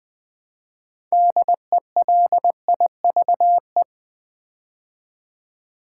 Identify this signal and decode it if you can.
Morse code, heard as a keyed tone, about 20 wpm, 690 Hz. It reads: DELIVE
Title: Build A Word - 3 Letters At A Time - 8-Letter Words - 20wpm